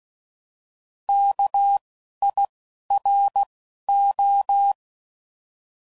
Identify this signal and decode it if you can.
Morse code, heard as a keyed tone, about 16 wpm, 800 Hz. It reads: K I R O